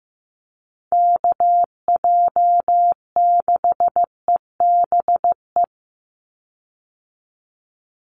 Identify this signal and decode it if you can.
Morse code, heard as a keyed tone, about 15 wpm, 700 Hz. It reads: KJ6EBE
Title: USA Call Signs 15wpm